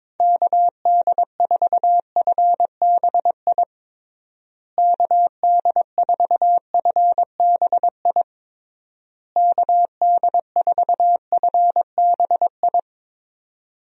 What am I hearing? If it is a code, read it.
Morse code, heard as a keyed tone, about 22 wpm, 700 Hz. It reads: KD4FBI KD4FBI KD4FBI
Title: USA Call Signs - 6 Characters - Repeated 3x - 22wpm